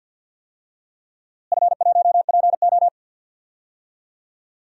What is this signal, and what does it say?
Morse code, heard as a keyed tone, about 52 wpm, 700 Hz. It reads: F1PO
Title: DXCC-100 Call Signs - 4 Characters - Repeated 2x - 50wpm